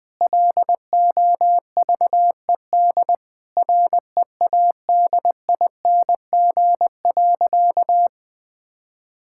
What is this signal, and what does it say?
Morse code, heard as a keyed tone, about 21 wpm, 700 Hz. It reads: LOVED READING.
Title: Sentences with 8-Word Phrases - 20wpm